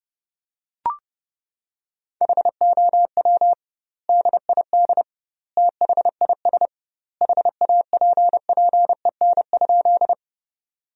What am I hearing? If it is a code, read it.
Morse code, heard as a keyed tone, about 32 wpm, 700 Hz. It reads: HOW DID THIS HAPPEN?
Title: 2x Word-Spacing - Sentences from Top 400 Words - 30wpm